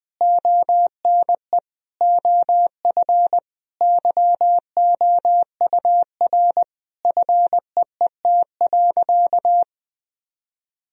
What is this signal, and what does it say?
Morse code, heard as a keyed tone, about 20 wpm, 700 Hz. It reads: ONE OF YOUR FEET.
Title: Sentences from Top 1000 Words - Review Part 1 - 20wpm